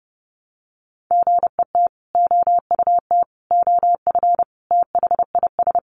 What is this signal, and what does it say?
Morse code, heard as a keyed tone, about 30 wpm, 700 Hz. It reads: GET OUT OF THIS
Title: Sentences from Top 1000 Words 30wpm